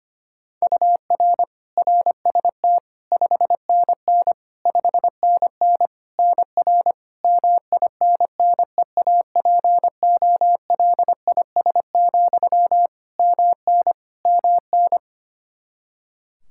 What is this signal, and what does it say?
Morse code, heard as a keyed tone, about 25 wpm, 700 Hz. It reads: UR RST 5NN 5NN NR MINNEAPOLIS, MN MN
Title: Practice QSO - Exchange 1 - Beginning - 25wpm